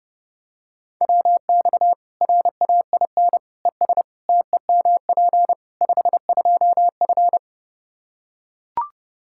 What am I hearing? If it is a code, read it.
Morse code, heard as a keyed tone, about 32 wpm, 700 Hz. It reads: WX RAIN ES TEMP 52F
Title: Practice QSO - Exchange 2 - Info - 30wpm